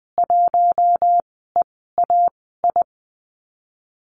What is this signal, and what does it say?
Morse code, heard as a keyed tone, about 20 wpm, spaced as for 15 wpm, 700 Hz. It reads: 1EAI
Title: 042 Lesson 8g - Number 1 - Call Signs 20wpm